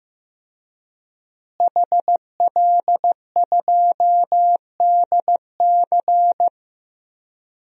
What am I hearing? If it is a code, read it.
Morse code, heard as a keyed tone, about 15 wpm, 700 Hz. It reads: HL2DC
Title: DXCC-100 Call Signs - 5 Characters - Repeated 2x - 15wpm